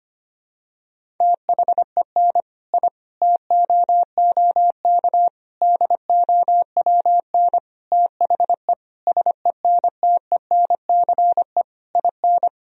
Morse code, heard as a keyed tone, about 25 wpm, 700 Hz. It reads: THEN I TOOK DOWN THE SENTENCE IN
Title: Sentences from Top 1000 Words - Review Part 1 - 25wpm